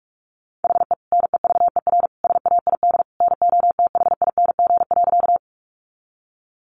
Morse code, heard as a keyed tone, about 44 wpm, 700 Hz. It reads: HE NEVER SAID NOTHING.
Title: Sentences from Top 500 Words 45wpm